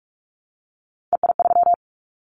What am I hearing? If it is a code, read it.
Morse code, heard as a keyed tone, about 46 wpm, 700 Hz. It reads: EI3
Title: Callsign Prefixes for Top 50 DXCC - 45wpm